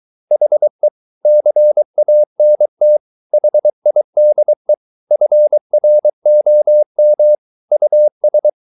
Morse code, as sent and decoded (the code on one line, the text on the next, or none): .... / -.-. .- -. - / .... .. -.. / ..-. .-. --- -- / ..- ...
HE CANT HIDE FROM US